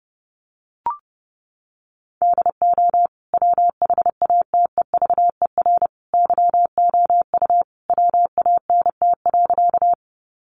Morse code, heard as a keyed tone, about 28 wpm, 700 Hz. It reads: DO WHATEVER YOU WANT.
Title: Sentences from Top 1100 Words 30wpm